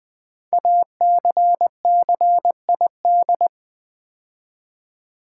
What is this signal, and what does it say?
Morse code, heard as a keyed tone, about 20 wpm, 700 Hz. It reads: ACCID
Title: Build A Word - 5 Letters At A Time - 12-Letter Words - 20wpm